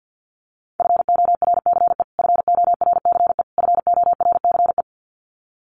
Morse code, heard as a keyed tone, about 50 wpm, 700 Hz. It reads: FORCE FORCE FORCE
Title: Top 500 Words Repeated 50wpm